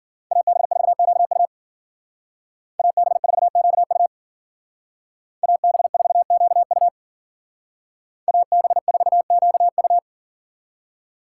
Morse code, heard as a keyed tone, about 48 wpm, 700 Hz. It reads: AB4QU AB4QU AB4QU AB4QU
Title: Speed Racing - USA 2x2 Call Signs 40wpm